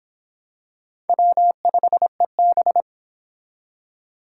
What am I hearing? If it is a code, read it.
Morse code, heard as a keyed tone, about 25 wpm, 700 Hz. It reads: W5EB